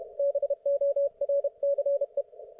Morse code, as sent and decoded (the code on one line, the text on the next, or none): -... --- .-. -.-. .
BORCE